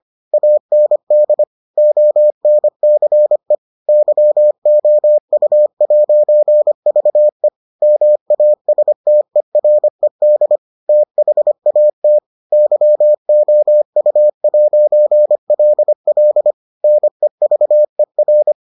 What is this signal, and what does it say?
Morse code, heard as a keyed tone, about 25 wpm, 600 Hz. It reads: AND ONCE YOU'VE MASTERED THAT YOU'LL NEVER